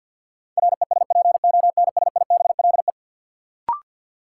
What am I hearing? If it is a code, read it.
Morse code, heard as a keyed tone, about 50 wpm, 700 Hz. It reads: RESPONSIBLE
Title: Sentences from Top 1000 Words - Review Part 7 - New Word Drill - 50wpm